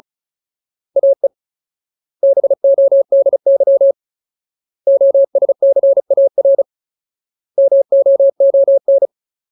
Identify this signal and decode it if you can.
Morse code, heard as a keyed tone, about 34 wpm, 550 Hz. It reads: AE BODY OSCAR MOON
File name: Ultimate Word List / 35WPM Ultimate